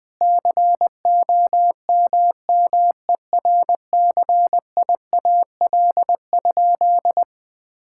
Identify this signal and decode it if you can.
Morse code, heard as a keyed tone, about 19 wpm, 700 Hz. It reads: COMMERCIAL?
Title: Sentences from Top 1000 Words - Review Part 5 - 20wpm